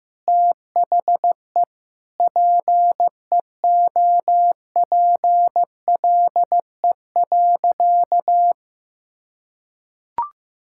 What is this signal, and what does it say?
Morse code, heard as a keyed tone, about 15 wpm, 700 Hz. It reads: THE PEOPLE.
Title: Sentences from Top 1000 Words - Review Part 5 - 15wpm